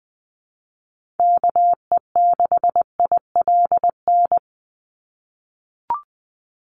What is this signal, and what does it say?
Morse code, heard as a keyed tone, about 20 wpm, 700 Hz. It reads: KE6ILN
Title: USA Call Signs 20wpm